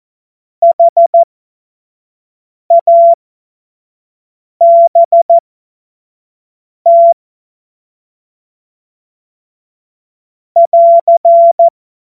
Morse code, heard as a keyed tone, about 14 wpm, spaced as for 5 wpm, 680 Hz. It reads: HABT <AR>
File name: Morsekurs / Witze / Witz 02. Februar 2021, 25 F70